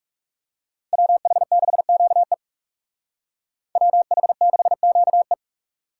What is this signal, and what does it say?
Morse code, heard as a keyed tone, about 37 wpm, 700 Hz. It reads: WH6QE WH6QE
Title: Speed Racing - USA 2x2 Call Signs 30wpm